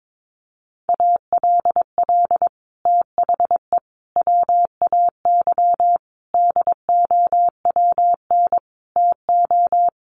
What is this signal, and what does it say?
Morse code, heard as a keyed tone, about 22 wpm, 700 Hz. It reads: ALL THE WAY DOWN TO
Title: Sets of 5 Words Encore2 22wpm